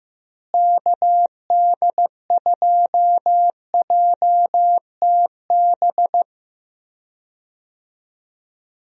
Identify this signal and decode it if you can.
Morse code, heard as a keyed tone, about 16 wpm, 700 Hz. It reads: KD2JTB
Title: USA Call Signs - 6 Characters - Repeated 3x - 15wpm